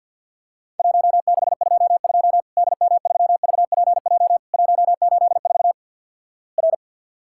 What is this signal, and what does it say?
Morse code, heard as a keyed tone, about 49 wpm, 700 Hz. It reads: 1622 DM3VPJ 184 R
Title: Worked All Europe DX Contest - QTC Exchange - 50wpm